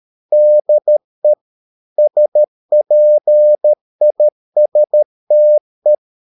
Morse code, heard as a keyed tone, about 13 wpm, 600 Hz. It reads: DE SPISTE
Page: Ferie 13 wpm